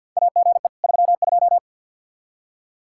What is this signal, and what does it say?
Morse code, heard as a keyed tone, about 50 wpm, 700 Hz. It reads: AGE 32
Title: K1USN SST QSO Practice 50wpm